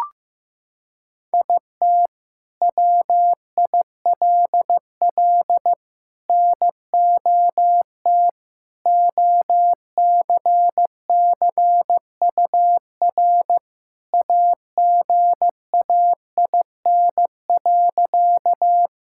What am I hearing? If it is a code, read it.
Morse code, heard as a keyed tone, about 14 wpm, 700 Hz. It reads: IT WILL NOT OCCUR AGAIN.
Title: Sentences from Top 1000 Words - Review Part 6 - 15wpm